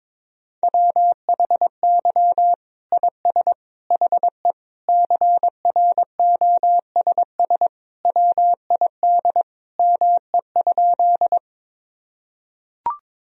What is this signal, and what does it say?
Morse code, heard as a keyed tone, about 22 wpm, 700 Hz. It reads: WHY IS HE CROSS WID ME?
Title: Sentences from Top 300 Words and Abbreviations 22wpm